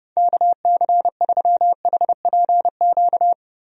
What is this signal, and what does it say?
Morse code, heard as a keyed tone, about 31 wpm, 700 Hz. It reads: KC3HPQ
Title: USA Call Signs - 6 Characters - 30wpm